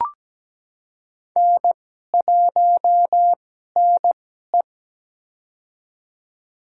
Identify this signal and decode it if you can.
Morse code, heard as a keyed tone, about 17 wpm, spaced as for 12 wpm, 700 Hz. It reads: N1NE
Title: USA Call Signs - 4 Characters - 2x Character Spacing - 17wpm